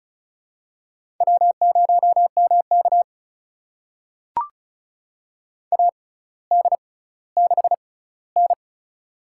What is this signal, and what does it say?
Morse code, heard as a keyed tone, about 36 wpm, spaced as for 18 wpm, 700 Hz. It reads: W0MK AD6N
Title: USA Call Signs - 4 Characters - 6x Character Spacing - 35wpm